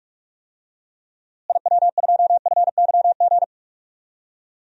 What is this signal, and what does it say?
Morse code, heard as a keyed tone, about 41 wpm, 700 Hz. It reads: IW2FYG
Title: DXCC-100 Call Signs - 6 Characters - Repeated 3x - 45wpm